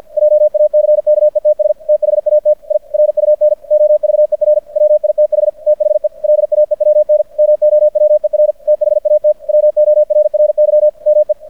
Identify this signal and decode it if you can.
Morse code, heard as a keyed tone, about 50 wpm, 600 Hz. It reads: JACKETS THAT I RUN OVER WITH THE LAEWN MOWER THAT WORRY ME
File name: archived / CRSnet